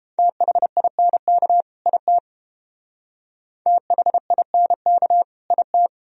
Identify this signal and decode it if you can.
Morse code, heard as a keyed tone, about 28 wpm, 700 Hz. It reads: THINK IT THINK IT